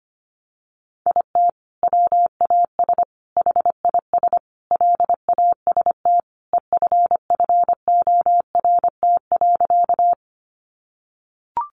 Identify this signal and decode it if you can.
Morse code, heard as a keyed tone, about 25 wpm, 700 Hz. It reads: IT WAS HIS LAST EFFORT.
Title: Sentences from Top 1200 Words 25wpm